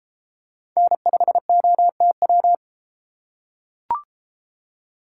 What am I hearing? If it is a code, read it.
Morse code, heard as a keyed tone, about 32 wpm, 700 Hz. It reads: N5OTW